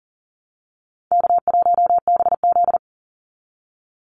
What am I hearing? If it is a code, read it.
Morse code, heard as a keyed tone, about 40 wpm, 700 Hz. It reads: K1BZ
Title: Easy Call Signs Repeated 3x - 40wpm